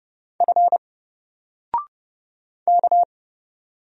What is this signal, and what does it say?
Morse code, heard as a keyed tone, about 30 wpm, 700 Hz. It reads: F K
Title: Single Letter-Number 30wpm